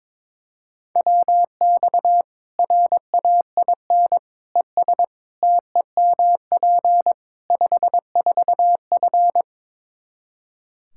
Morse code, heard as a keyed tone, about 22 wpm, 700 Hz. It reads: WX RAIN ES TEMP 54F
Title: Practice QSO - Exchange 2 - Info - 22wpm